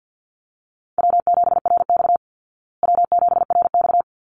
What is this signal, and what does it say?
Morse code, heard as a keyed tone, about 50 wpm, 700 Hz. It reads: W7RX W7RX
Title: Easy Call Signs Repeated 2x - 50wpm